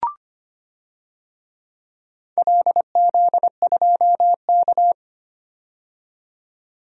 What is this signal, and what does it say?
Morse code, heard as a keyed tone, about 25 wpm, 700 Hz. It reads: LZ2K
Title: DXCC-100 Call Signs - 4 Characters - 25wpm